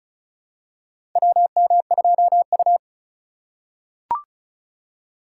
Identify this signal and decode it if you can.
Morse code, heard as a keyed tone, about 36 wpm, 700 Hz. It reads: WM2U
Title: USA Call Signs - 4x Character Spacing - 35wpm